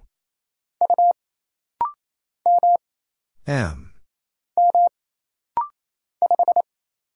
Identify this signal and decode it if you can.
Morse code, heard as a keyed tone, about 27 wpm, 700 Hz. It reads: U M M 5
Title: Single Letter-Number Alt - Rapid-Fire 28wpm